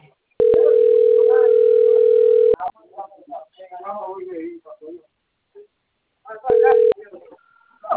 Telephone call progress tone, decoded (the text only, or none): ringback tone